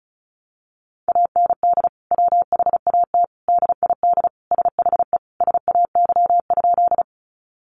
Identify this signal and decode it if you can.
Morse code, heard as a keyed tone, about 34 wpm, 700 Hz. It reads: AND WHAT DID SHE SAY?
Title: Sentences from Top 200 Words 35wpm